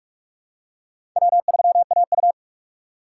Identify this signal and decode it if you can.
Morse code, heard as a keyed tone, about 46 wpm, 700 Hz. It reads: W3AU
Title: USA Call Signs - 3x Character Spacing - 45wpm